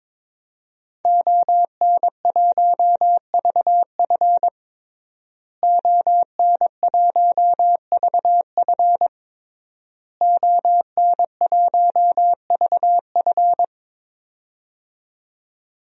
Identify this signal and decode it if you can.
Morse code, heard as a keyed tone, about 22 wpm, 700 Hz. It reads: ON1VF ON1VF ON1VF
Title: DXCC-100 Call Signs - 5 Characters - Repeated 3x - 22wpm